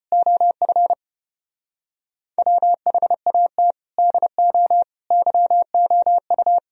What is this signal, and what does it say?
Morse code, heard as a keyed tone, about 29 wpm, 700 Hz. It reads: OF WHAT DO YOU
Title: Speed-Racing - Sets of 5 Words 25wpm